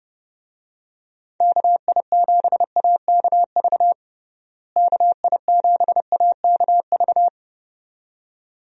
Morse code, heard as a keyed tone, about 32 wpm, 700 Hz. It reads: KI7AKV KI7AKV
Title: USA Call Signs - 6 Characters - Repeated 2x - 30wpm